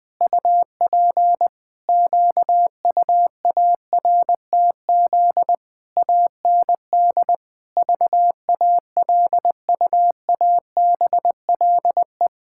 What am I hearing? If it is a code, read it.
Morse code, heard as a keyed tone, about 20 wpm, 700 Hz. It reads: UP QUARTZ AND VALUABLE
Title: Pangram Challenge v2 - 20wpm